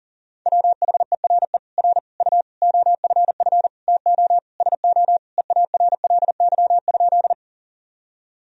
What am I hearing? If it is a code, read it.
Morse code, heard as a keyed tone, about 40 wpm, 700 Hz. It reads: WHERE R U OFF TO SO EARLY?
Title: Sentences from Top 400 Words and Abbreviations 40wpm